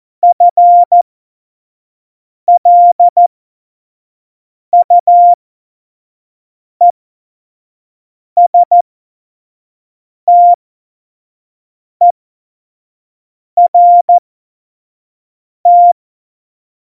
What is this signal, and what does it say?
Morse code, heard as a keyed tone, about 14 wpm, spaced as for 5 wpm, 700 Hz. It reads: FLUESTERT
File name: Morsekurs / Witze / Witz 06. Oktober 2020, 25 F70